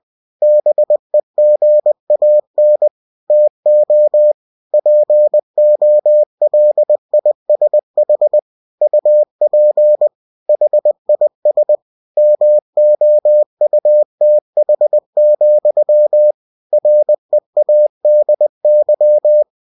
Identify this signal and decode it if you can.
Morse code, heard as a keyed tone, about 20 wpm, 600 Hz. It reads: BEGAN TO POLISH UP HIS MOUTH, READY
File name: cw-books / winnie-the-pooh / winnie-20wpm-page-0040